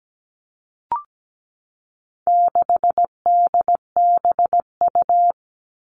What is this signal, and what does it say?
Morse code, heard as a keyed tone, about 17 wpm, 700 Hz. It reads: 6DBU
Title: All Cut Numbers - 4-Digit 17wpm